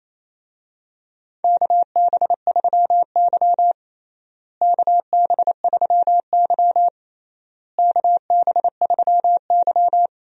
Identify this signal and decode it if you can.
Morse code, heard as a keyed tone, about 28 wpm, 700 Hz. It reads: KB3Y KB3Y KB3Y
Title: USA Call Signs - 4 Characters - Repeated 3x - 28wpm